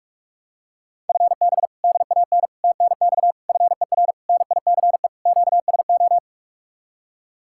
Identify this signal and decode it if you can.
Morse code, heard as a keyed tone, about 45 wpm, 700 Hz. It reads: FB DAN TNX FER NICE QSO